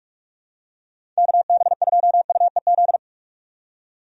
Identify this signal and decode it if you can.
Morse code, heard as a keyed tone, about 42 wpm, 700 Hz. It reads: KB2UEZ
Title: USA Call Signs - 6 Characters - 45wpm